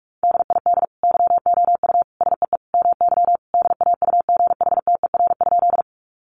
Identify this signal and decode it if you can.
Morse code, heard as a keyed tone, about 44 wpm, 700 Hz. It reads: DID YOU SEE MY DAUGHTER?
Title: Sentences from Top 1500 Words 45wpm